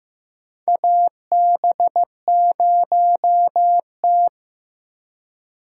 Morse code, heard as a keyed tone, about 15 wpm, 700 Hz. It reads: AB0T
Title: USA Call Signs - 4 Characters - 3x Character Spacing - 15wpm